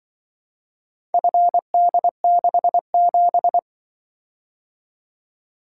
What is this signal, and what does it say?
Morse code, heard as a keyed tone, about 25 wpm, 700 Hz. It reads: FD67
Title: Speed-Racing - Four Letter-Number Combinations - 20wpm